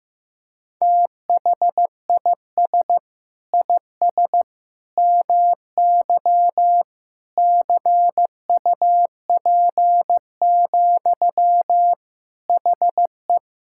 Morse code, heard as a keyed tone, about 15 wpm, 700 Hz. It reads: THIS IS MY CUP, HE
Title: Sentences from Top 1600 Words 15wpm